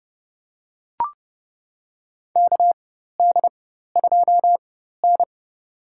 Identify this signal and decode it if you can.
Morse code, heard as a keyed tone, about 30 wpm, spaced as for 14 wpm, 700 Hz. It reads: KD2N
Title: USA Call Signs - 4 Characters - 4x Character Spacing - 30wpm